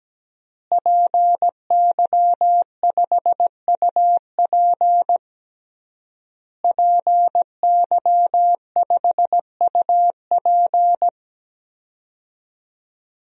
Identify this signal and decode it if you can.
Morse code, heard as a keyed tone, about 17 wpm, 700 Hz. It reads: PY5UP PY5UP
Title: DXCC-100 Call Signs - 5 Characters - Repeated 3x - 17wpm